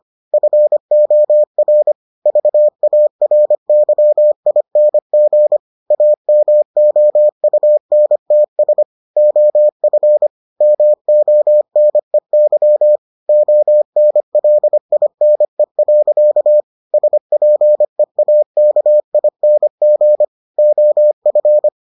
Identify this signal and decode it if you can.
Morse code, heard as a keyed tone, about 25 wpm, 600 Hz. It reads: FOR VARYING AMOUNTS OF MONEY ONLINE. SPEAKING OF